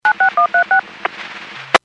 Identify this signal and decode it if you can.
DTMF, heard as telephone touch tones, tones of about 110 ms. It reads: 96136
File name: Businesses / 00865 13-34 Businesses 451.25 WPS